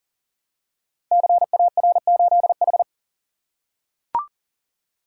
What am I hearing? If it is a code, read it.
Morse code, heard as a keyed tone, about 38 wpm, 700 Hz. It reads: CAR8H
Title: Code Groups - 5 Letter:Numbers - 40wpm